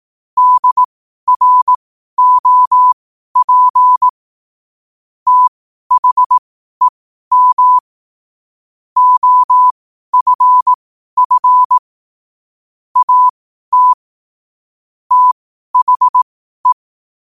Morse code, heard as a keyed tone, about 18 wpm, spaced as for 12 wpm, 1000 Hz. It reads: DROP THEM OFF AT THE
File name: qotdfiles / 12wpm